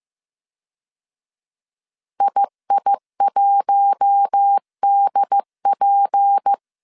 Morse code, heard as a keyed tone, about 15 wpm, 800 Hz. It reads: II1DP
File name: CW / DET / 1 6 1500C S C 14wpm 2951